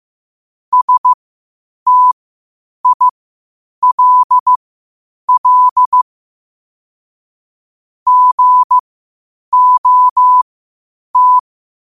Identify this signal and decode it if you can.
Morse code, heard as a keyed tone, about 15 wpm, spaced as for 8 wpm, 1000 Hz. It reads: STILL GOT
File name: qotdfiles / 8wpm